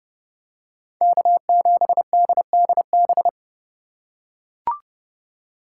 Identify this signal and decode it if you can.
Morse code, heard as a keyed tone, about 29 wpm, 700 Hz. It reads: K7DDB